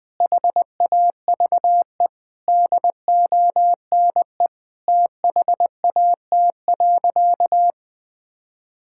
Morse code, heard as a keyed tone, about 20 wpm, 700 Hz. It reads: HAVE DONE THAT.
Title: Sentences from Top 1600 Words 20wpm